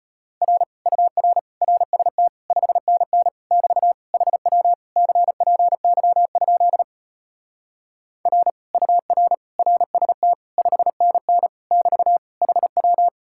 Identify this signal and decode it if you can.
Morse code, heard as a keyed tone, about 33 wpm, 700 Hz. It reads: R UR RST 5NN <BT> HW CPY? R UR RST 5NN <BT> HW